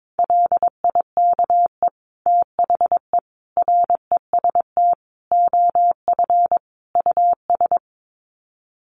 Morse code, heard as a keyed tone, about 22 wpm, 700 Hz. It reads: LIKE THE REST OF US
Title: Sets of 5 Words Encore2 22wpm